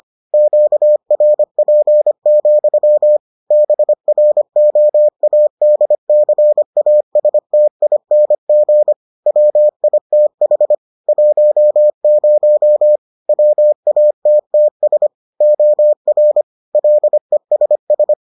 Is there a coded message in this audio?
Yes: Morse code from 0.3 to 18.1 s, heard as a keyed tone, about 25 wpm, 600 Hz. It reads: QRP, BROADCASTING WITH 10 WATTS OR LESS